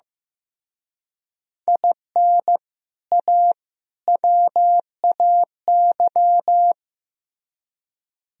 Morse code, heard as a keyed tone, about 18 wpm, 700 Hz. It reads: IN A WAY
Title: Speed-Racing - Sets of 3 Words 15wpm